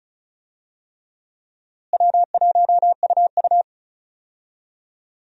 Morse code, heard as a keyed tone, about 34 wpm, 700 Hz. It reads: W1UU
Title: USA Call Signs - 4 Characters - Repeated 3x - 35wpm